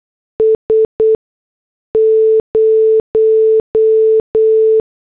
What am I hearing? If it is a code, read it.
Morse code, heard as a keyed tone, about 8 wpm, 440 Hz. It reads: S 0
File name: morse / headlines